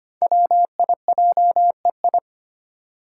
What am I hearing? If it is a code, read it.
Morse code, heard as a keyed tone, about 25 wpm, 700 Hz. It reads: WIJEI
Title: Five Challenging Letter-Number Combinations 25wpm